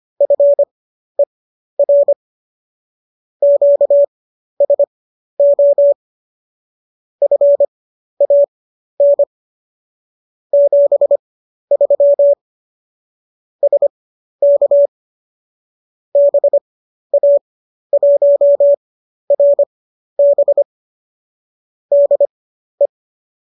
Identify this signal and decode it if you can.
Morse code, heard as a keyed tone, about 25 wpm, spaced as for 12 wpm, 570 Hz. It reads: FER QSO FAN 73 SK BA1RB DE